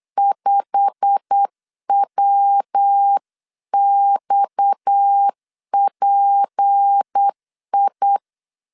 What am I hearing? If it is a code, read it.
Morse code, heard as a keyed tone, about 8 wpm, 800 Hz. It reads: 5WXPI
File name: 8wpm